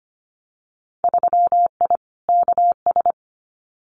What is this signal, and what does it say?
Morse code, heard as a keyed tone, about 25 wpm, 700 Hz. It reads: 3I KS
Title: Winter Field Day Exchanges 25wpm